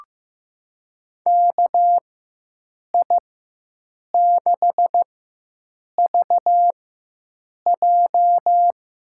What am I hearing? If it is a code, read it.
Morse code, heard as a keyed tone, about 16 wpm, spaced as for 7 wpm, 700 Hz. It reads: KI6VJ
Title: USA Call Signs - 6 Characters - 4x Character Spacing - 15wpm